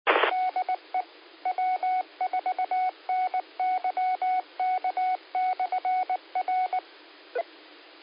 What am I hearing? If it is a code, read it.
Morse code, heard as a keyed tone, about 19 wpm, 740 Hz. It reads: DE W4NYK/R
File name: qsorec SimplexLogic 15-20-42